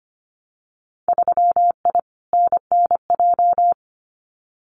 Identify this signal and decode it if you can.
Morse code, heard as a keyed tone, about 25 wpm, 700 Hz. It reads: 3I NNJ
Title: Winter Field Day Exchanges 25wpm